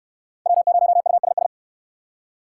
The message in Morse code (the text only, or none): W0USS